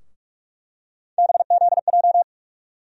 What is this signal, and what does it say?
Morse code, heard as a keyed tone, about 44 wpm, 700 Hz. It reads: BZJ
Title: USA Call Sign Postfixes - 3 Letters 45wpm